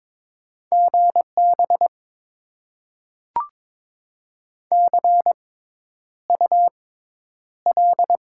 Morse code, heard as a keyed tone, about 22 wpm, spaced as for 10 wpm, 700 Hz. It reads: GB CUL
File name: Top 90 QSO Elements - 6x Character Spacing - 22wpm